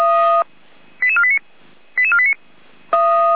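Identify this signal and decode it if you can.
Five-tone selective call (ZVEI), heard as quick runs of five tones, tones of about 80 ms. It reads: 88489 88489
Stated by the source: ZVEI five-tone call received by SDR, from Feuerwehr Orlamünde audio server.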